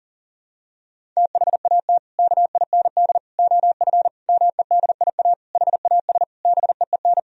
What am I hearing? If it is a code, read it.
Morse code, heard as a keyed tone, about 37 wpm, 700 Hz. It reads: THAT KIND OF MEDIA HAS BEEN
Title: Sentences from Top 1000 Words - Review Part 4 - 40wpm